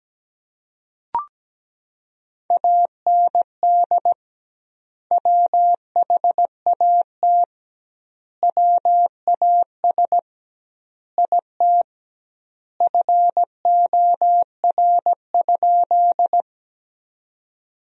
Morse code, heard as a keyed tone, about 17 wpm, 700 Hz. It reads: AND WHAT WAS IT FOR?
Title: 2x Word-Spacing - Sentences from Top 100 Words - 17wpm